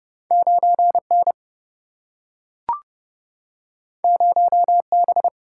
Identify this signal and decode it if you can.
Morse code, heard as a keyed tone, about 29 wpm, 700 Hz. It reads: 9 N 0 B
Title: Single-Letter Single-Number 30wpm